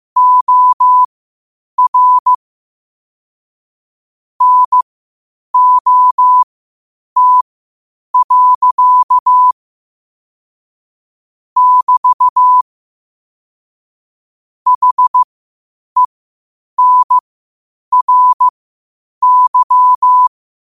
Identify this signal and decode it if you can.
Morse code, heard as a keyed tone, about 15 wpm, spaced as for 8 wpm, 1000 Hz. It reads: OR NOT. <BT> HENRY
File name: qotdfiles / 8wpm